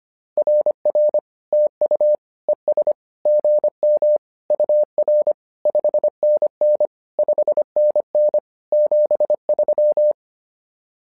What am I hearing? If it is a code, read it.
Morse code, heard as a keyed tone, about 25 wpm, 600 Hz. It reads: RR TU ES GM UR 5NN 5NN 73
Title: SOTA QSO Practice 25wpm